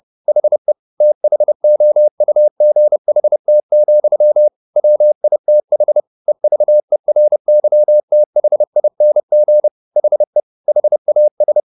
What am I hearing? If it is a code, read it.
Morse code, heard as a keyed tone, about 30 wpm, 600 Hz. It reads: HE THOUGHT, WITH EVERYTHING HE HAS
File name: the old man and the sea / the old man and the sea 1 30wpm